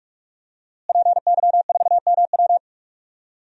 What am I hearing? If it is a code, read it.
Morse code, heard as a keyed tone, about 46 wpm, 700 Hz. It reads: PY4MW
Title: DXCC-100 Call Signs - 5 Characters - Repeated 3x - 45wpm